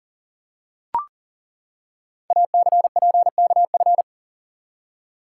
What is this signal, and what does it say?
Morse code, heard as a keyed tone, about 41 wpm, 700 Hz. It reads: ACPKF
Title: Code Groups - 5 Letter:Numbers - 40wpm